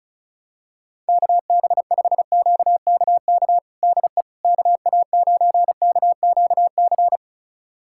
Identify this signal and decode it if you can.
Morse code, heard as a keyed tone, about 36 wpm, 700 Hz. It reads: KB5QKK DE KA9KQC
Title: USA Callsign DE Callsign - 6 Characters - 35wpm